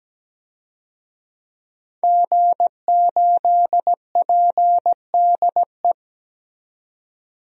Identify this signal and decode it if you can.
Morse code, heard as a keyed tone, about 17 wpm, 700 Hz. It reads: G8PDE
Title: DXCC-100 Call Signs - 5 Characters - 17wpm